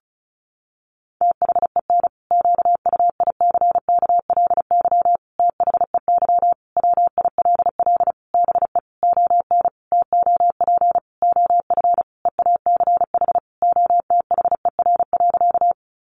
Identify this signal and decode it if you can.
Morse code, heard as a keyed tone, about 34 wpm, 700 Hz. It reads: THEN QUICKLY THEY WILL BE ON TOP OF EACH OTHER.
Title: Sentences from Top 700 Words 35wpm